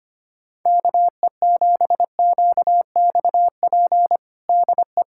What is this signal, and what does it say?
Morse code, heard as a keyed tone, about 25 wpm, 700 Hz. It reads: KE7QXP DE
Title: USA Callsign DE Callsign 25wpm